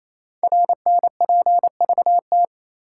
Morse code, heard as a keyed tone, about 28 wpm, 700 Hz. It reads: RNPVT
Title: Code Groups - 5 Letters - 28wpm